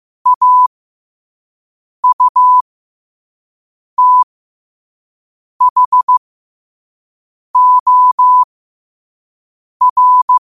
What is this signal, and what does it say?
Morse code, heard as a keyed tone, about 15 wpm, spaced as for 5 wpm, 1000 Hz. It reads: AUTHOR
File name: 5wpm